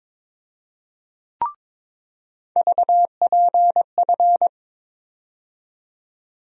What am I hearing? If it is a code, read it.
Morse code, heard as a keyed tone, about 22 wpm, 700 Hz. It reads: VPF
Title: USA Call Sign Postfixes - 3 Letters 22wpm